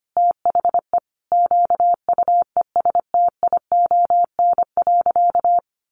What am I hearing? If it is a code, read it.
Morse code, heard as a keyed tone, about 25 wpm, 700 Hz. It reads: THE QUESTION.
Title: Sentences from Top 600 Words 25wpm